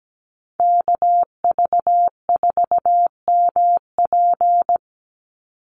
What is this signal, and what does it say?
Morse code, heard as a keyed tone, about 17 wpm, 700 Hz. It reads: KV4MP
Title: USA Call Signs 17wpm